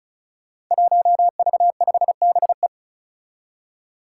Morse code, heard as a keyed tone, about 34 wpm, 700 Hz. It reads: 1V5BE
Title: Five Challenging Letter-Number Combinations 35wpm